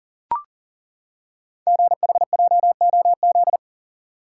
Morse code, heard as a keyed tone, about 43 wpm, 700 Hz. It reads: GHJOZ
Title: Code Groups - 5 Letters - 40wpm